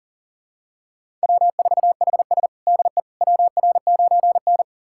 Morse code, heard as a keyed tone, about 40 wpm, 700 Hz. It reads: W4HS DE WR9N
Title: USA Callsign DE Callsign - 4 Characters - 40wpm